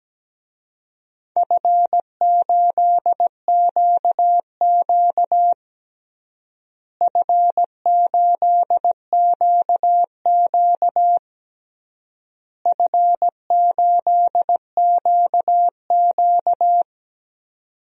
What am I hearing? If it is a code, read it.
Morse code, heard as a keyed tone, about 17 wpm, 700 Hz. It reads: F8QQ F8QQ F8QQ
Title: DXCC-100 Call Signs - 4 Characters - Repeated 3x - 17wpm